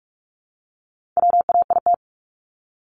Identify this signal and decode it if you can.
Morse code, heard as a keyed tone, about 44 wpm, 700 Hz. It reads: WAIT
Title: ING Suffix - Top 100 Words with Base Words 45wpm